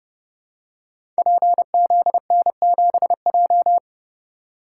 Morse code, heard as a keyed tone, about 30 wpm, 700 Hz. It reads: PZN7J
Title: Code Groups - 5 Letter:Numbers - 30wpm